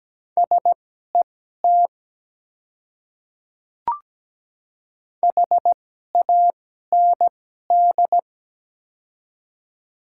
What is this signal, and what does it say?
Morse code, heard as a keyed tone, about 17 wpm, spaced as for 12 wpm, 700 Hz. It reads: SET HAND
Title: 2x Character Spacing - Top 200 Words - 17wpm